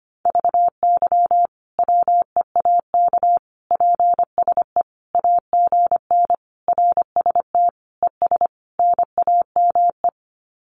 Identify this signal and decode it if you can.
Morse code, heard as a keyed tone, about 25 wpm, 700 Hz. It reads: VY WEAK PSE AGN RST ES NAME
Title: Common QSO Phrases 25wpm